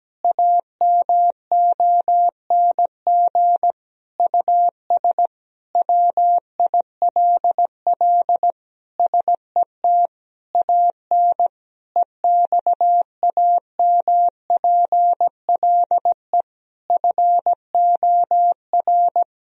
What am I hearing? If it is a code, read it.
Morse code, heard as a keyed tone, about 17 wpm, 700 Hz. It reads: AMONG US WILL SET AN EXAMPLE FOR
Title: Sentences from Top 1000 Words - Review Part 3 - 17wpm